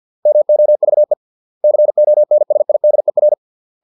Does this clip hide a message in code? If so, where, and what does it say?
Yes: Morse code from 0.3 to 3.3 s, heard as a keyed tone, about 50 wpm, 600 Hz. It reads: MOVE CONSIDER